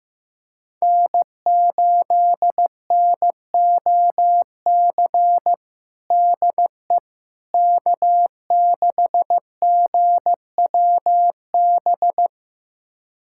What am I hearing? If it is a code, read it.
Morse code, heard as a keyed tone, about 15 wpm, 700 Hz. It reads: N8NOC DE K6GWB
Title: USA Callsign DE Callsign - 5 Characters - 15wpm